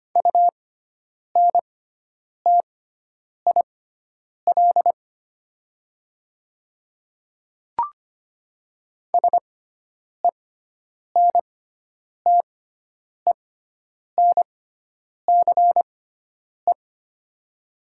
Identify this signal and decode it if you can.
Morse code, heard as a keyed tone, about 25 wpm, spaced as for 9 wpm, 700 Hz. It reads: UNTIL SENTENCE
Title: 6x Character Spacing - Top 300 Words - 25wpm